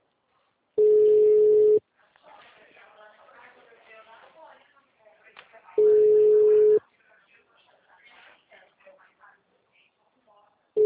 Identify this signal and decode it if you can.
Telephone call progress tone (Europe, 425 Hz), heard as a telephone network's tone, about 420 Hz, with speech. Ringback tone.